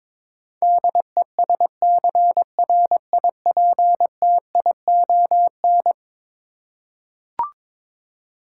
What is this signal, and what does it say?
Morse code, heard as a keyed tone, about 22 wpm, 700 Hz. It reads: DESCRIPTION
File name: practice / 22wpm / Eleven Letter Words - 22wpm